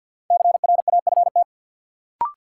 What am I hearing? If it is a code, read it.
Morse code, heard as a keyed tone, about 53 wpm, 700 Hz. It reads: CRAFT
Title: Sentences from Top 1000 Words - Review Part 6 - New Word Drill - 50wpm